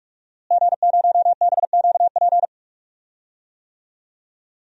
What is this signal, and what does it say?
Morse code, heard as a keyed tone, about 45 wpm, 700 Hz. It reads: G0BQP
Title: DXCC-100 Call Signs - 5 Characters - Repeated 3x - 45wpm